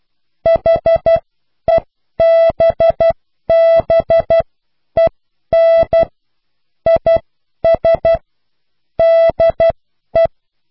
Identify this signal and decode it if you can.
Morse code, heard as a keyed tone, about 12 wpm, spaced as for 10 wpm, 660 Hz. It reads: HEBBEN IS DE